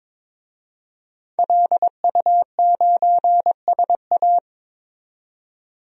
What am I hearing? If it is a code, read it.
Morse code, heard as a keyed tone, about 22 wpm, 700 Hz. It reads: LU9SA
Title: DXCC-100 Call Signs - 5 Characters - 22wpm